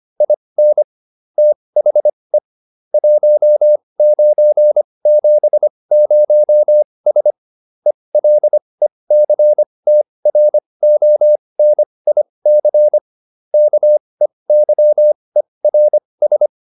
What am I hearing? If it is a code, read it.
Morse code, heard as a keyed tone, about 25 wpm, spaced as for 20 wpm, 600 Hz. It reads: IN THE 1970S ELECTRONIC KEYERS
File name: SS315 20